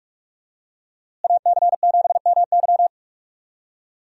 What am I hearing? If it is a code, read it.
Morse code, heard as a keyed tone, about 44 wpm, 700 Hz. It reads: AC7MY
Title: USA Call Signs - 5 Characters - Repeated 2x - 45wpm